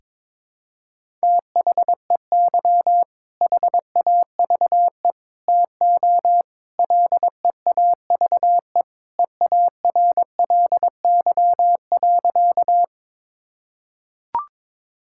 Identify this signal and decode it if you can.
Morse code, heard as a keyed tone, about 21 wpm, 700 Hz. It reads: THEY HAVE TO LEAVE EARLY.
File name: practice / 22wpm / Sentences with 3-Word Phrases - 22wpm